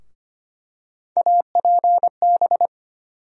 Morse code, heard as a keyed tone, about 25 wpm, 700 Hz. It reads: APB
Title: USA Call Sign Postfixes - 3 Letters 25wpm